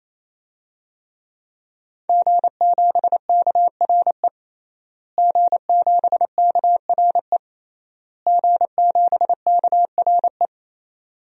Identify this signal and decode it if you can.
Morse code, heard as a keyed tone, about 28 wpm, 700 Hz. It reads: G7KRE G7KRE G7KRE
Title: DXCC-100 Call Signs - 5 Characters - Repeated 3x - 28wpm